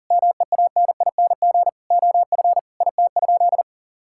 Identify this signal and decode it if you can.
Morse code, heard as a keyed tone, about 40 wpm, 700 Hz. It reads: MEANING OF IT?
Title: Sentences from Top 1000 Words - Review Part 6 - 40wpm